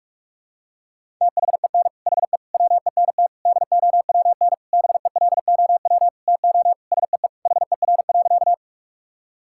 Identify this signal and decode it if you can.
Morse code, heard as a keyed tone, about 41 wpm, 690 Hz. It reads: THEN HE WENT DOWN BELOW TO SEE HER.
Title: Sentences from Top 1000 Words - Review Part 5 - 45wpm